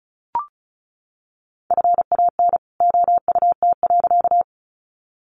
Morse code, heard as a keyed tone, about 34 wpm, 700 Hz. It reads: FAN OUT.
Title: Sentences from Top 1300 Words 35wpm